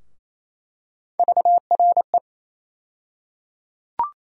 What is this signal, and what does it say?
Morse code, heard as a keyed tone, about 28 wpm, 700 Hz. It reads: VRE
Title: USA Call Sign Postfixes - 3 Letters 28wpm